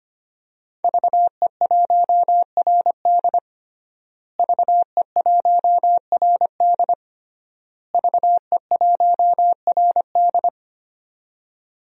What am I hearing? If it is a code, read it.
Morse code, heard as a keyed tone, about 25 wpm, 700 Hz. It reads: VE1RD VE1RD VE1RD
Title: DXCC-100 Call Signs - 5 Characters - Repeated 3x - 25wpm